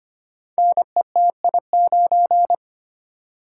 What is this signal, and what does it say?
Morse code, heard as a keyed tone, about 25 wpm, 700 Hz. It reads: NETI9
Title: Code Groups - 5 Letter:Numbers - 25wpm